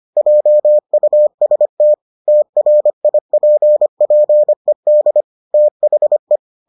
Morse code, heard as a keyed tone, about 25 wpm, 600 Hz. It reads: JUST TRIPPED THE